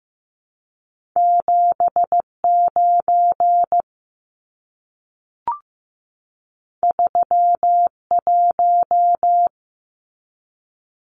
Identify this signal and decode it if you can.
Morse code, heard as a keyed tone, about 15 wpm, 700 Hz. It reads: 7 9 3 1
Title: Numbers - 2-Digit 15wpm